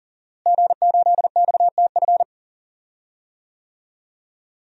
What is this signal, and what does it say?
Morse code, heard as a keyed tone, about 40 wpm, 700 Hz. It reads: G8XTF
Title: DXCC-100 Call Signs - 5 Characters - Repeated 3x - 40wpm